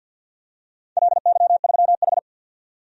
Morse code, heard as a keyed tone, about 53 wpm, 700 Hz. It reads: LY3H
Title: DXCC-100 Call Signs - 4 Characters - 50wpm